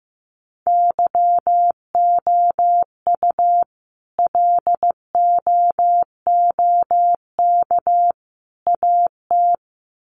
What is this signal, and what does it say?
Morse code, heard as a keyed tone, about 15 wpm, 700 Hz. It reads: YOU LOOK AT
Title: Sets of 3 Words 15wpm